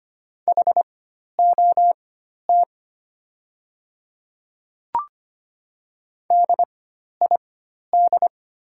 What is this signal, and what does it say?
Morse code, heard as a keyed tone, about 25 wpm, spaced as for 12 wpm, 690 Hz. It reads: HOT DID
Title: 4x Character Spacing - Top 100 Words - 25wpm